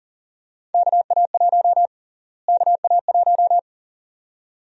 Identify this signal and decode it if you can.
Morse code, heard as a keyed tone, about 40 wpm, 700 Hz. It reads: KA1 KA1
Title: USA Call Sign Prefixes - Repeated 2x - 40wpm